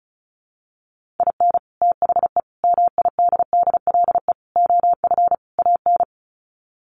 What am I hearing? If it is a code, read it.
Morse code, heard as a keyed tone, about 34 wpm, 700 Hz. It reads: IN THE MIDDLE OF AN
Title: Sets of 5 Words Encore2 35wpm